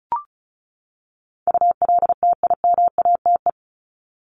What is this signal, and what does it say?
Morse code, heard as a keyed tone, about 34 wpm, 700 Hz. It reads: ULTIMATE
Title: LY Suffix - Top 100 Words with Base Words 35wpm